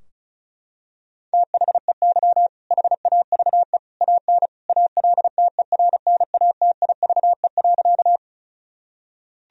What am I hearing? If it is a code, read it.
Morse code, heard as a keyed tone, about 36 wpm, 700 Hz. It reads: THEY HAVE AN ALTERNATIVE.
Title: Sentences from Top 1000 Words - Review Part 4 - 35wpm